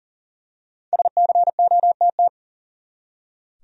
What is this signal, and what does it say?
Morse code, heard as a keyed tone, about 39 wpm, 700 Hz. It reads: SCOTT
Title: USA Common Names - Max 5 Letters - 40wpm